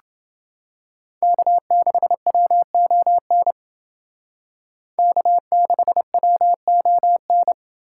Morse code, heard as a keyed tone, about 29 wpm, 700 Hz. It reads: K6WON K6WON